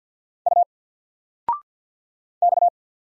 Morse code, heard as a keyed tone, about 50 wpm, 700 Hz. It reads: U X
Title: Single Letter-Number 50wpm